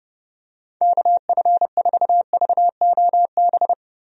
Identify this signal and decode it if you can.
Morse code, heard as a keyed tone, about 30 wpm, 700 Hz. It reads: KF4VOB
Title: USA Call Signs - 6 Characters - 30wpm